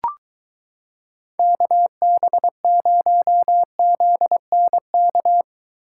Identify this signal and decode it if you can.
Morse code, heard as a keyed tone, about 23 wpm, 700 Hz. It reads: KB0ZNK